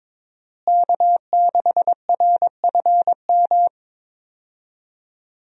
Morse code, heard as a keyed tone, about 21 wpm, 700 Hz. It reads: K6RFM
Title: USA Call Signs - 5 Characters - Repeated 2x - 22wpm